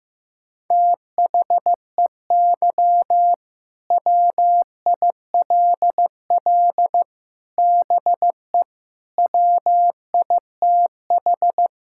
Morse code, heard as a keyed tone, about 15 wpm, 700 Hz. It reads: THEY WILL BE WITH
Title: Sentences from Top 1600 Words 15wpm